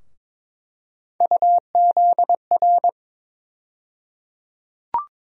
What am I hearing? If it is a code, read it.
Morse code, heard as a keyed tone, about 22 wpm, 700 Hz. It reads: UZR
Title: USA Call Sign Postfixes - 3 Letters 22wpm